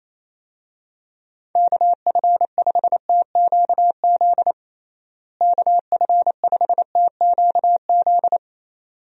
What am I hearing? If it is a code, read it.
Morse code, heard as a keyed tone, about 28 wpm, 700 Hz. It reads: KF5TQZ KF5TQZ
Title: USA Call Signs - 6 Characters - Repeated 2x - 28wpm